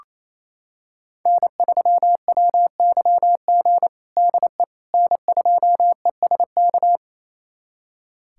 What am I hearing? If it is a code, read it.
Morse code, heard as a keyed tone, about 28 wpm, 700 Hz. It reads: N3WYG DE N2ESK
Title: USA Callsign DE Callsign - 5 Characters - 28wpm